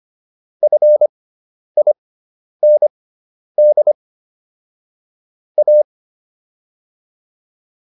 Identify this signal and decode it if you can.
Morse code, heard as a keyed tone, about 25 wpm, spaced as for 10 wpm, 600 Hz. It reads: FIND A